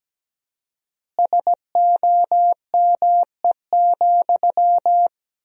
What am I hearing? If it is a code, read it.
Morse code, heard as a keyed tone, about 17 wpm, 700 Hz. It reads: SOME,
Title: 4x Word-Spacing - Sentences from Top 200 Words - 17wpm